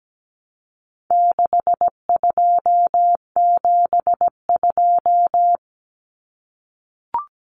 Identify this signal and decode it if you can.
Morse code, heard as a keyed tone, about 17 wpm, 700 Hz. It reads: 6272
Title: Numbers - 4-Digit 17wpm